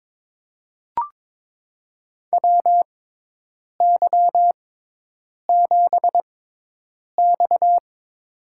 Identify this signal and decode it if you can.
Morse code, heard as a keyed tone, about 22 wpm, spaced as for 8 wpm, 700 Hz. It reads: WY7X
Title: USA Call Signs - 4 Characters - 6x Character Spacing - 22wpm